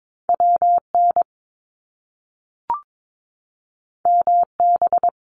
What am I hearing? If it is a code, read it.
Morse code, heard as a keyed tone, about 22 wpm, 700 Hz. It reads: WN MB
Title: Top Two Letter Combinations in Words 22wpm